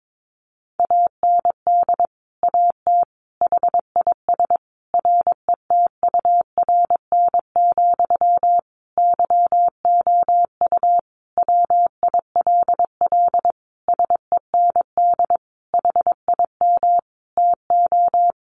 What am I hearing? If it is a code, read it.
Morse code, heard as a keyed tone, about 22 wpm, 700 Hz. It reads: AND AT HIS RETURN, YOU WILL SEND HIM TO
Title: Sentences from Top 1500 Words 22wpm